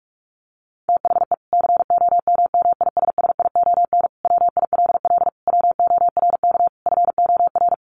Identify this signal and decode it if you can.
Morse code, heard as a keyed tone, about 44 wpm, 700 Hz. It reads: THE COMMISSION WILL WORK FOR
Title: Sentences from Top 1400 Words 45wpm